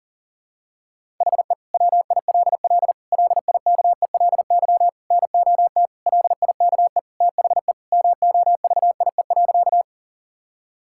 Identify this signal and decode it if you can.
Morse code, heard as a keyed tone, about 40 wpm, 700 Hz. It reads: HE WILL LIKELY NOT LIKE THE MOVIE.
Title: Sentences from Top 1000 Words - Review Part 5 - 40wpm